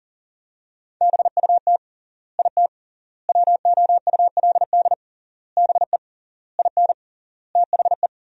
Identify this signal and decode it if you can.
Morse code, heard as a keyed tone, about 40 wpm, 700 Hz. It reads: BUT IT WOULD BE IN THE
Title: 3x Word-Spacing - Sentences from Top 300 Words - 40wpm